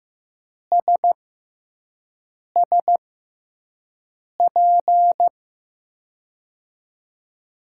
Morse code, heard as a keyed tone, about 15 wpm, spaced as for 5 wpm, 700 Hz. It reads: SSP
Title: USA Call Signs - 6 Characters - 6x Character Spacing - 15wpm